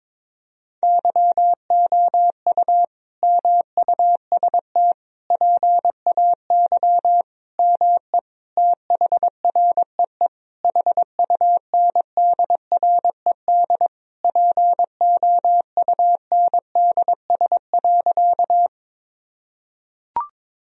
Morse code, heard as a keyed tone, about 22 wpm, 700 Hz. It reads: YOU MUST PAY ME THREE HUNDRED POUNDS.
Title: Sentences from Top 1000 Words - Review Part 2 - 22wpm